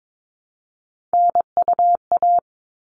Morse code, heard as a keyed tone, about 22 wpm, 700 Hz. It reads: NUA
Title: All Cut Numbers - 3-Digit 22wpm